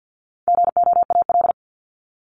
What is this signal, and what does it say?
Morse code, heard as a keyed tone, about 52 wpm, 700 Hz. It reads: GOAL
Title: Top 751-1000 Words 50wpm